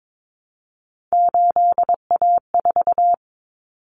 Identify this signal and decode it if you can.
Morse code, heard as a keyed tone, about 22 wpm, 700 Hz. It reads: 8A4
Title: Common Cut Numbers - 3-Digit 22wpm